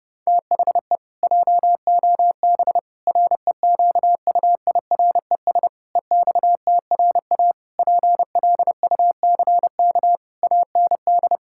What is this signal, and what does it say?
Morse code, heard as a keyed tone, about 30 wpm, 700 Hz. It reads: THE JOB REQUIRES EXTRA PLUCK AND
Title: Pangram Challenge v2 - 30wpm